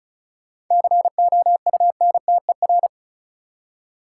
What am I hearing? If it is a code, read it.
Morse code, heard as a keyed tone, about 36 wpm, 700 Hz. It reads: COUNTER
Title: Seven Letter Words - 35wpm